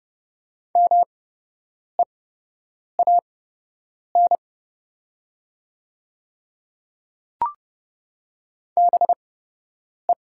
Morse code, heard as a keyed tone, about 31 wpm, spaced as for 8 wpm, 700 Hz. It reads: MEAN BE
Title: 8x Character Spacing - Top 200 Words - 30wpm